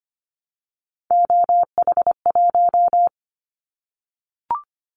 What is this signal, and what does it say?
Morse code, heard as a keyed tone, about 25 wpm, 700 Hz. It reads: OH1
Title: Callsign Prefixes for Top 50 DXCC - 25wpm